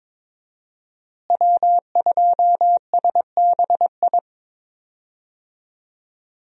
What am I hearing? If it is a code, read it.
Morse code, heard as a keyed tone, about 21 wpm, 700 Hz. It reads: W2SBI
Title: USA Call Signs - 5 Characters - Repeated 3x - 22wpm